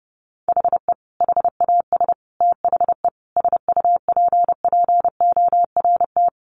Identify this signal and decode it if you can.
Morse code, heard as a keyed tone, about 30 wpm, 700 Hz. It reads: HE HAS THE SUPPORT
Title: Sentences from Top 800 Words 30wpm